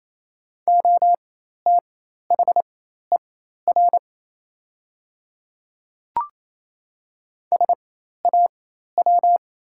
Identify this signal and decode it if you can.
Morse code, heard as a keyed tone, about 27 wpm, spaced as for 13 wpm, 700 Hz. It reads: OTHER SAW
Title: 4x Character Spacing - Top 300 Words - 28wpm